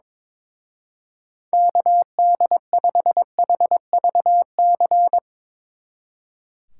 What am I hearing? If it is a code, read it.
Morse code, heard as a keyed tone, about 24 wpm, 700 Hz. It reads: KD5HVC
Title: Speed Racing - USA 2x3 Call Signs 22wpm